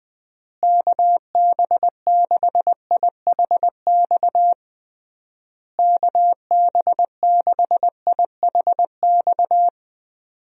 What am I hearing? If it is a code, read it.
Morse code, heard as a keyed tone, about 21 wpm, 700 Hz. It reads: KB6IHX KB6IHX
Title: USA Call Signs - 6 Characters - Repeated 3x - 20wpm